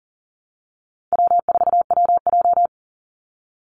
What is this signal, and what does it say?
Morse code, heard as a keyed tone, about 40 wpm, 700 Hz. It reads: W4WJ
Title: Easy Call Signs Repeated 3x - 40wpm